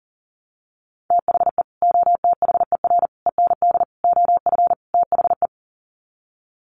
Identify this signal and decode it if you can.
Morse code, heard as a keyed tone, about 40 wpm, 700 Hz. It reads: THE OTHER END OF THE
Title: Sets of 5 Words 40wpm